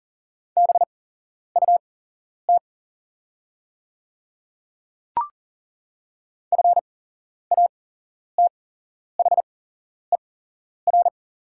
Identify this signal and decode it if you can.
Morse code, heard as a keyed tone, about 42 wpm, spaced as for 11 wpm, 700 Hz. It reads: BUT FATHER
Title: 8x Character Spacing - Top 300 Words - 40wpm